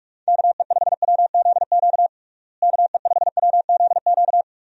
Morse code, heard as a keyed tone, about 46 wpm, 700 Hz. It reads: KE5WZQ KE5WZQ
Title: USA Call Signs - 6 Characters - Repeated 2x - 45wpm